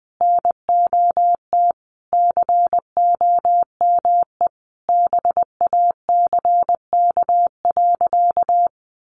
Morse code, heard as a keyed tone, about 20 wpm, 700 Hz. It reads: NOT COME BACK.
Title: Sentences from Top 1300 Words 20wpm